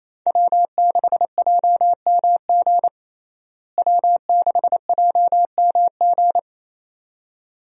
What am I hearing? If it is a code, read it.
Morse code, heard as a keyed tone, about 28 wpm, 700 Hz. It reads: W6JMG W6JMG
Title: USA Call Signs - 5 Characters - Repeated 2x - 28wpm